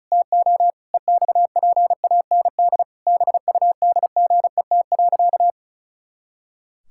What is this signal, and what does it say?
Morse code, heard as a keyed tone, about 34 wpm, 700 Hz. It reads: TO EXPAND BUDGET.
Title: Pangram Challenge v2 - 35wpm